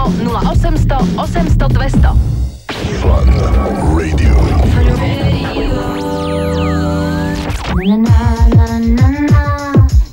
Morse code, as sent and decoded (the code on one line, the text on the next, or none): none